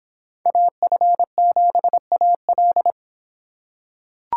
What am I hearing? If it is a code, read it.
Morse code, heard as a keyed tone, about 26 wpm, 700 Hz. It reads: AF7AL